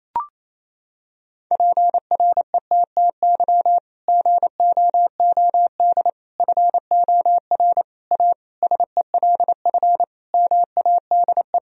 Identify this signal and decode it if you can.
Morse code, heard as a keyed tone, about 27 wpm, 700 Hz. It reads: PRETTY GOOD FOR A SELF MADE
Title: Sentences from Top 1000 Words - Review Part 1 - 28wpm